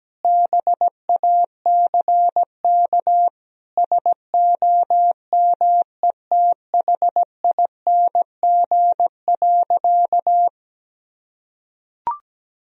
Morse code, heard as a keyed tone, about 17 wpm, 700 Hz. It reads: BACK SOMETHING.
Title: Sentences from Top 1000 Words - Review Part 3 - 17wpm